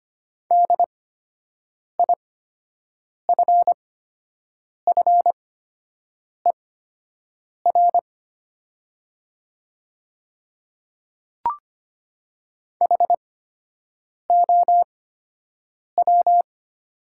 Morse code, heard as a keyed tone, about 25 wpm, spaced as for 7 wpm, 700 Hz. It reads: DIFFER HOW
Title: 8x Character Spacing - Top 200 Words - 25wpm